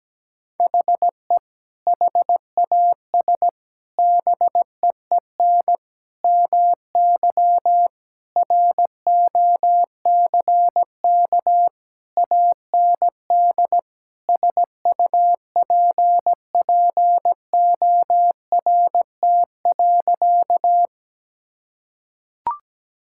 Morse code, heard as a keyed tone, about 17 wpm, 700 Hz. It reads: HE HAS BEEN MY ROCK AND SUPPORT.
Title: Sentences from Top 1000 Words - Review Part 5 - 17wpm